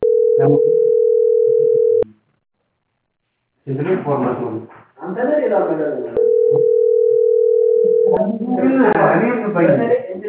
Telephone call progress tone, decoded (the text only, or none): ringback tone